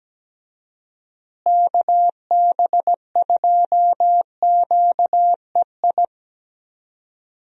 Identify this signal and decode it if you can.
Morse code, heard as a keyed tone, about 17 wpm, 700 Hz. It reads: KB2QEI
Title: USA Call Signs - 6 Characters - Repeated 2x - 17wpm